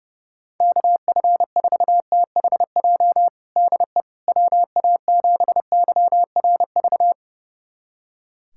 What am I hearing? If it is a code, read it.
Morse code, heard as a keyed tone, about 30 wpm, 700 Hz. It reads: KF4THJ DE WA7YRV
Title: USA Callsign DE Callsign 30wpm